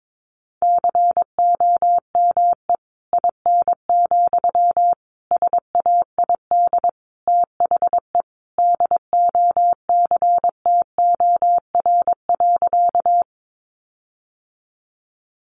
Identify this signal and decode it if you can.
Morse code, heard as a keyed tone, about 22 wpm, 700 Hz. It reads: COME IN, SAID THE DOCTOR.
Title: Sentences from Top 1500 Words 22wpm